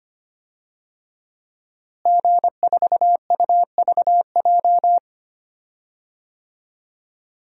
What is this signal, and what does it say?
Morse code, heard as a keyed tone, about 25 wpm, 700 Hz. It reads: G4UVJ
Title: DXCC-100 Call Signs - 5 Characters - Repeated 3x - 25wpm